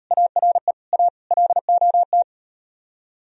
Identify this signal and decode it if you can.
Morse code, heard as a keyed tone, about 38 wpm, 700 Hz. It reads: ARE A LOT